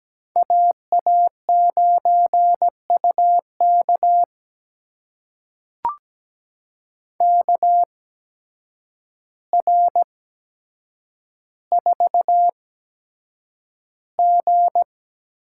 Morse code, heard as a keyed tone, about 17 wpm, spaced as for 8 wpm, 700 Hz. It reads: AA9UK KR4G
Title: USA Call Signs - 5 Characters - 8x Character Spacing - 17wpm